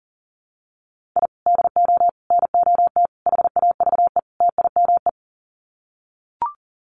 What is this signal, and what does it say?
Morse code, heard as a keyed tone, about 40 wpm, 700 Hz. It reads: I DO NOT HAVE TIME
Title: Sets of 5 Words Encore 40wpm